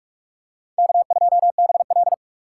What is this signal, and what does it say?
Morse code, heard as a keyed tone, about 40 wpm, 700 Hz. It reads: K2BL